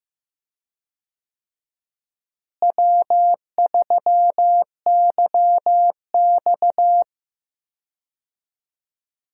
Morse code, heard as a keyed tone, about 15 wpm, 700 Hz. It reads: W3YX
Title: USA Call Signs - 4 Characters - Repeated 3x - 15wpm